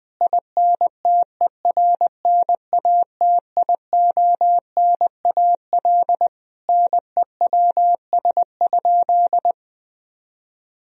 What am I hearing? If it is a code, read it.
Morse code, heard as a keyed tone, about 20 wpm, 700 Hz. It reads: INTERNATIONAL NEWS?
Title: Sentences from Top 1000 Words - Review Part 4 - 20wpm